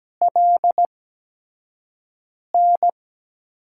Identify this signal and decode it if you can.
Morse code, heard as a keyed tone, about 17 wpm, spaced as for 5 wpm, 700 Hz. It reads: LN
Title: USA Call Signs - 5 Characters - 8x Character Spacing - 17wpm